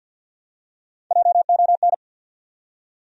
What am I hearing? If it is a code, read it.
Morse code, heard as a keyed tone, about 48 wpm, 700 Hz. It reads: JON
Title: USA Common Names - Max 3 Letters - 50wpm